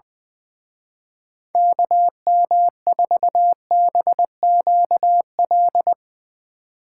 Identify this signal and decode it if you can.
Morse code, heard as a keyed tone, about 22 wpm, 700 Hz. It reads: KM4BQL
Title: Speed Racing - USA 2x3 Call Signs 20wpm